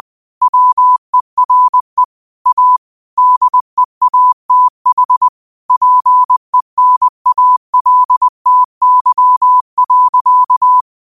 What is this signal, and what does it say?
Morse code, heard as a keyed tone, about 20 wpm, 1000 Hz. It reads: WERE A DEATH PENALTY.
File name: 20wpm